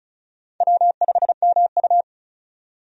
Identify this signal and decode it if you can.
Morse code, heard as a keyed tone, about 34 wpm, 700 Hz. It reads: W5MU
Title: USA Call Signs - 4 Characters - 35wpm